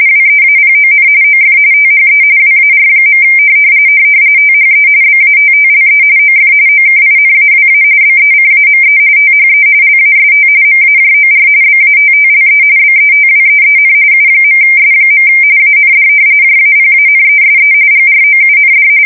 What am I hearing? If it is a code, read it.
RTTY, about 45.45 baud. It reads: WEAPON OF THE WAR. THE BATTLE OF OKINAWA BECAME THE LAST MAJOR BATTLE BETWEEN U.S. AND JAPANESE GROUND
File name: fsk170 / usnavy4